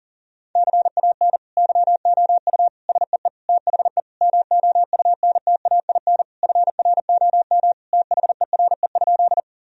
Morse code, heard as a keyed tone, about 40 wpm, 700 Hz. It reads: CAN YOU SEE THE MOUNTAIN FROM THERE?
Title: Sentences from Top 1000 Words - Review Part 1 - 40wpm